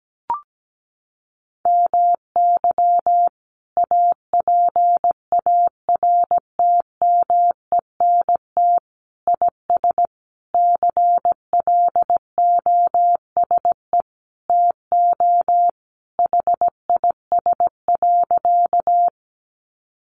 Morse code, heard as a keyed tone, about 17 wpm, 700 Hz. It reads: MY APARTMENT IS CLOSE TO HIS.
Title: Sentences from Top 900 Words 17wpm